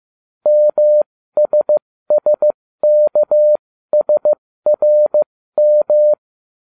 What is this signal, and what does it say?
Morse code, heard as a keyed tone, about 15 wpm, spaced as for 13 wpm, 600 Hz. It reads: MSSKSRM